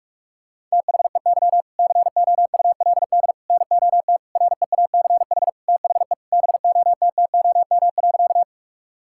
Morse code, heard as a keyed tone, about 42 wpm, 700 Hz. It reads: THEY COULD NOT REACH THE BOTTOM.
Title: Sentences from Top 1000 Words - Review Part 6 - 45wpm